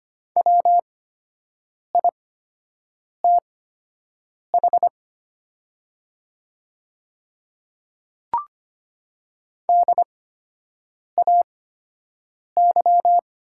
Morse code, heard as a keyed tone, about 25 wpm, spaced as for 7 wpm, 700 Hz. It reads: WITH DAY
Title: 8x Character Spacing - Top 100 Words - 25wpm